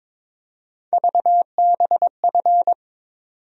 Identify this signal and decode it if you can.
Morse code, heard as a keyed tone, about 22 wpm, 700 Hz. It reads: VBF
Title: USA Call Sign Postfixes - 3 Letters 22wpm